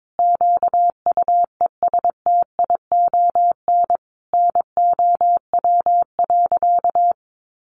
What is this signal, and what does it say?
Morse code, heard as a keyed tone, about 22 wpm, 700 Hz. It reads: QUESTION NOW.
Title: Sentences from Top 600 Words 22wpm